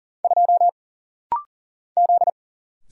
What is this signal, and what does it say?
Morse code, heard as a keyed tone, about 40 wpm, 700 Hz. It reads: 2 Z
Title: Single Letter-Number Alt - Rapid-Fire 40wpm